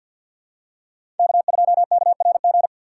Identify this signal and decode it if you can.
Morse code, heard as a keyed tone, about 52 wpm, 700 Hz. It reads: K2KRG
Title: USA Call Signs - 5 Characters - 3x Character Spacing - 50wpm